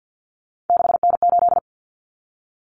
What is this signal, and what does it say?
Morse code, heard as a keyed tone, about 50 wpm, 700 Hz. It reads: 6N8
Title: Common Cut Numbers - 3-Digit 50wpm